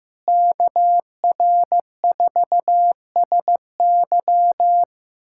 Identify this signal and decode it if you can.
Morse code, heard as a keyed tone, about 15 wpm, 700 Hz. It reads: KR4SY
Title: USA Callsign DE Callsign - 5 Characters - 15wpm